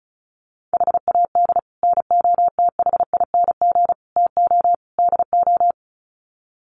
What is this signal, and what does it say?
Morse code, heard as a keyed tone, about 34 wpm, 700 Hz. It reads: HAD NOTHING TO DO
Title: Sets of 4 Words 35wpm